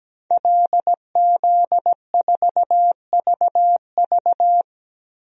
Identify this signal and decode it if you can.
Morse code, heard as a keyed tone, about 17 wpm, 700 Hz. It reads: LZ4VV
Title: DXCC-100 Call Signs - 5 Characters - Repeated 2x - 17wpm